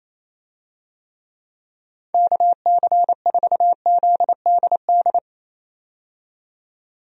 Morse code, heard as a keyed tone, about 27 wpm, 700 Hz. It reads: KC4ZDD
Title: USA Call Signs - 6 Characters - 28wpm